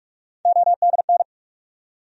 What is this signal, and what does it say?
Morse code, heard as a keyed tone, about 46 wpm, 700 Hz. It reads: ODN
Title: USA Call Sign Postfixes - 3 Letters 45wpm